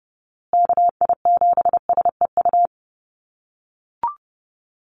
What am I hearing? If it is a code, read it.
Morse code, heard as a keyed tone, about 30 wpm, 700 Hz. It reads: KI7SEU
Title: USA Call Signs Repeated 3x 30wpm